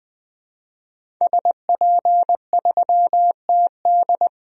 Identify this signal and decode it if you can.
Morse code, heard as a keyed tone, about 20 wpm, 700 Hz. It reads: SP3TD
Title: DXCC-100 Call Signs - 5 Characters - Repeated 3x - 20wpm